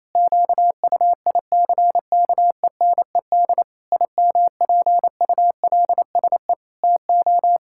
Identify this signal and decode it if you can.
Morse code, heard as a keyed tone, about 27 wpm, 700 Hz. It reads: QUICKENED IMPULSE TO
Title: Pangram Challenge v2 - 28wpm